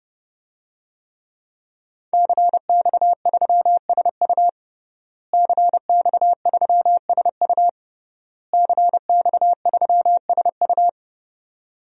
Morse code, heard as a keyed tone, about 32 wpm, 700 Hz. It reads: CX3SU CX3SU CX3SU
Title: DXCC-100 Call Signs - 5 Characters - Repeated 3x - 30wpm